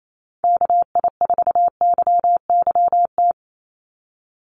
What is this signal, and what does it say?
Morse code, heard as a keyed tone, about 28 wpm, 700 Hz. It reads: KI4YYT
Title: USA Call Signs 28wpm